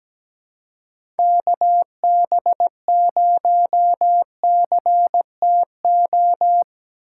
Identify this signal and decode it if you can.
Morse code, heard as a keyed tone, about 17 wpm, 700 Hz. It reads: KB0CTO
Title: USA Call Signs - 6 Characters - 3x Character Spacing - 17wpm